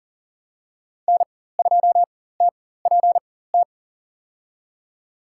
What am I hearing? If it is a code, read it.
Morse code, heard as a keyed tone, about 40 wpm, spaced as for 19 wpm, 700 Hz. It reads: N2TPT
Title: USA Call Signs - 4x Character Spacing - 40wpm